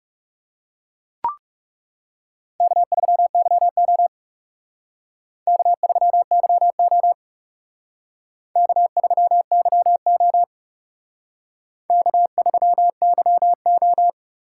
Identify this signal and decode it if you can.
Morse code, heard as a keyed tone, about 36 wpm, 700 Hz. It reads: K3YO K3YO K3YO K3YO
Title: Speed Racing - USA 1x2 Call Signs 30wpm